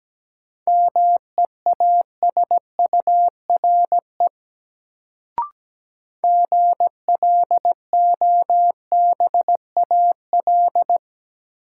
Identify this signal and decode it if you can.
Morse code, heard as a keyed tone, about 17 wpm, 700 Hz. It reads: MEASURE GLOBAL
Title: Sentences from Top 1200 Words - New Word Drill Alt - 17wpm